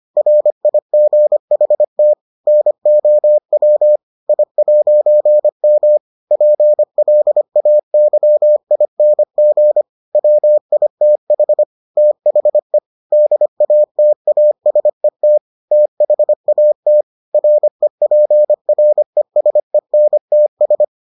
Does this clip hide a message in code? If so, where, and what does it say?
Yes: Morse code from 0.2 to 20.8 s, heard as a keyed tone, about 25 wpm, 600 Hz. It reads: RIGHT NOW I'M PLAYING WITH THE DATASET THAT REPRESENTS